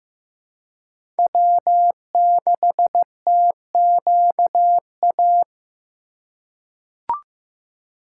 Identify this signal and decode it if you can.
Morse code, heard as a keyed tone, about 15 wpm, 700 Hz. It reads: W6TQA
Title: USA Call Signs - 4x Character Spacing - 15wpm